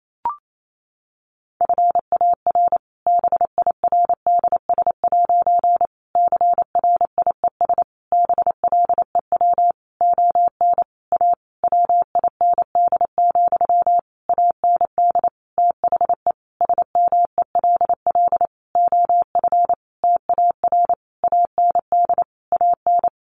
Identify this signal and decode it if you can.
Morse code, heard as a keyed tone, about 28 wpm, 700 Hz. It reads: FAR BIRDS' CRIES BLEW ON A WIND, AND THE SMELL OF TAR AND AN
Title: A Sound Of Thunder 28wpm